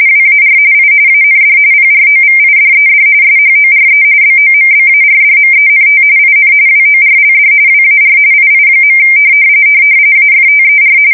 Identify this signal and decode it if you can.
RTTY, about 45.45 baud. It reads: WAR I AND PROVIDED SHORE BOMBARDMENT IN WORLD WAR II. AFTER
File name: fsk170 / USS-New-Jersey